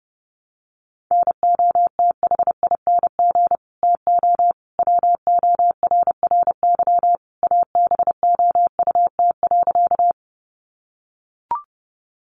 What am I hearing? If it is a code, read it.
Morse code, heard as a keyed tone, about 30 wpm, 700 Hz. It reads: NOTHING TO WORRY ABOUT.
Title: Sentences from Top 1100 Words 30wpm